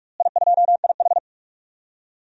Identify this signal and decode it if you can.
Morse code, heard as a keyed tone, about 46 wpm, 700 Hz. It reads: I2IH
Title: DXCC-100 Call Signs - 4 Characters - Repeated 2x - 45wpm